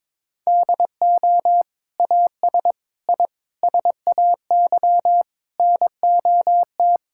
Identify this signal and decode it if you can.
Morse code, heard as a keyed tone, about 22 wpm, 700 Hz. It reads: DO AS I SAY NOT
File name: practice / 22wpm / Sets of 7 Words - 22wpm